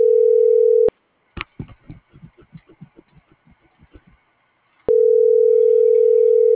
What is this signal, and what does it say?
Ringback tone (North America), 440 + 480 Hz.